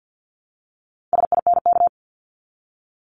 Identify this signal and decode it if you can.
Morse code, heard as a keyed tone, about 50 wpm, 700 Hz. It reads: SINK
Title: Multi-letter Phonemes Repeated 3x 50wpm